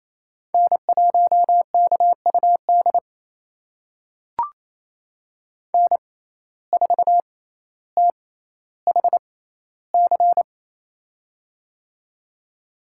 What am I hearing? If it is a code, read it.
Morse code, heard as a keyed tone, about 28 wpm, spaced as for 14 wpm, 700 Hz. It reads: N1KUD N4THC
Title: USA Call Signs - 5 Characters - 6x Character Spacing - 28wpm